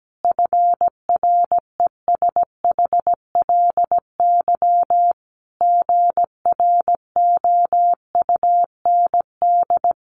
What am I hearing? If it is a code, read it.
Morse code, heard as a keyed tone, about 17 wpm, 700 Hz. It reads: FRESHLY GROUND
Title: Sets of 5 Words Encore 17wpm